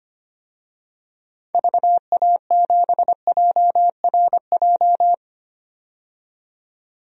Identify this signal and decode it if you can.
Morse code, heard as a keyed tone, about 25 wpm, 690 Hz. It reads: VA7JRJ
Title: DXCC-100 Call Signs - 6 Characters - 25wpm